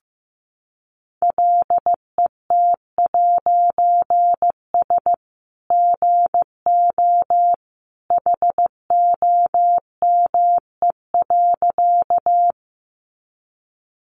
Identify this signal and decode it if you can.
Morse code, heard as a keyed tone, about 15 wpm, 700 Hz. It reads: LET'S GO HOME.
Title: A Sound Of Thunder 15wpm